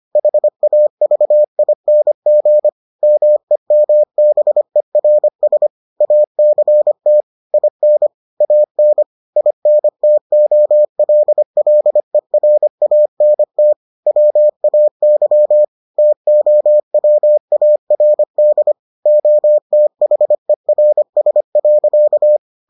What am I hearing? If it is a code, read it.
Morse code, heard as a keyed tone, about 25 wpm, 600 Hz. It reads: HAVING MEMBERS ACT IN AN INTOLLERANT WAY TOWARD OTHERS.